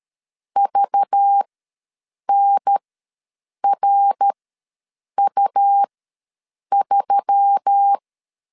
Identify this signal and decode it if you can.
Morse code, heard as a keyed tone, about 13 wpm, spaced as for 7 wpm, 800 Hz. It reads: VNRU3
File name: LEZIONI / 5 LBVC / 5 LBV ws 12wpm 3015